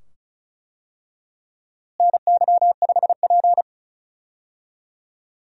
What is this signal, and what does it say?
Morse code, heard as a keyed tone, about 36 wpm, 700 Hz. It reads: NY5P
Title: USA Call Signs - 4 Characters - Repeated 2x - 35wpm